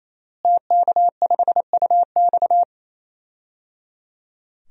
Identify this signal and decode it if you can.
Morse code, heard as a keyed tone, about 28 wpm, 700 Hz. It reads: TK5UX
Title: DXCC-100 Call Signs - 5 Characters - Repeated 3x - 28wpm